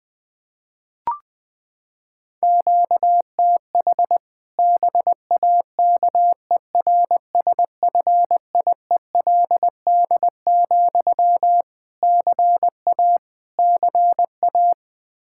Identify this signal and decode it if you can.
Morse code, heard as a keyed tone, about 21 wpm, 700 Hz. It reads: QTH BAKERSFIELD, CA CA
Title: Practice QSO - Exchange 1 - Beginning - 20wpm